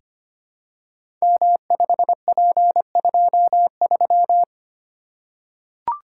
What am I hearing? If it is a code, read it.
Morse code, heard as a keyed tone, about 25 wpm, 700 Hz. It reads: M5P23
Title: Code Groups - 5 Letter:Numbers - 25wpm